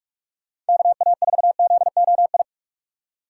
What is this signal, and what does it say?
Morse code, heard as a keyed tone, about 57 wpm, 700 Hz. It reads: KA4ZOI